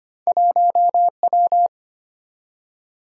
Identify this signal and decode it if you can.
Morse code, heard as a keyed tone, about 25 wpm, 700 Hz. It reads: 1W
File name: Two Letter-Number Combinations - 25wpm